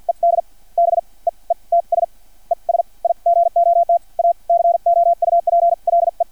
Morse code, heard as a keyed tone, about 50 wpm, spaced as for 38 wpm, 680 Hz. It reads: ED BEETS ES IGOT A COUPLE
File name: archived / CRSnet